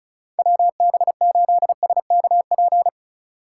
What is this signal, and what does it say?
Morse code, heard as a keyed tone, about 36 wpm, 700 Hz. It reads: WB8SKP
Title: K1USN SST QSO Practice - 35wpm